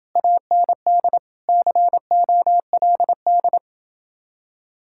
Morse code, heard as a keyed tone, about 29 wpm, 700 Hz. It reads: AND COLD